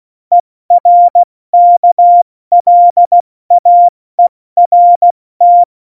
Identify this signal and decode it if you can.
Morse code, heard as a keyed tone, about 16 wpm, 700 Hz. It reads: ERKLAERT
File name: Morsekurs / Witze / Witz 11. Juni 2019, 70 F80